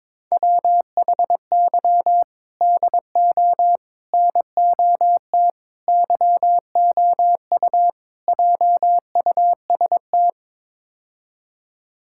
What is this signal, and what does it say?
Morse code, heard as a keyed tone, about 21 wpm, 700 Hz. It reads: WHY DO NOT YOU JUST
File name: practice / 22wpm / Build A Phrase - 5-Word Phrases - 22wpm